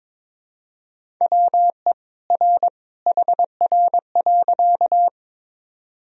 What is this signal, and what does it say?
Morse code, heard as a keyed tone, about 21 wpm, 700 Hz. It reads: WE R HR.
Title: Sentences from Top 200 Words and Abbreviations 22wpm